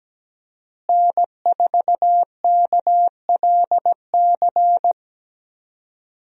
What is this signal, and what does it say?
Morse code, heard as a keyed tone, about 17 wpm, 700 Hz. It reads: N4KLC
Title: USA Call Signs - 5 Characters - Repeated 2x - 17wpm